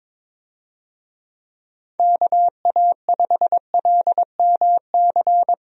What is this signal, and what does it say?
Morse code, heard as a keyed tone, about 22 wpm, 700 Hz. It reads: KA5LMC
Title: USA Call Signs - 6 Characters - 22wpm